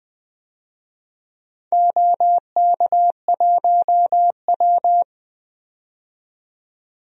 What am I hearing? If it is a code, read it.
Morse code, heard as a keyed tone, about 20 wpm, 700 Hz. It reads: OK1W
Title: DXCC-100 Call Signs - 4 Characters - 20wpm